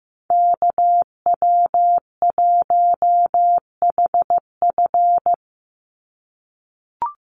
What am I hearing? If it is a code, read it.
Morse code, heard as a keyed tone, about 15 wpm, 700 Hz. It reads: KW1HF
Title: USA Call Signs 15wpm